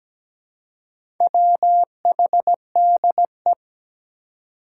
Morse code, heard as a keyed tone, about 17 wpm, 700 Hz. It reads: WHDE
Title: Four Challenging Letter-Number Combinations 17wpm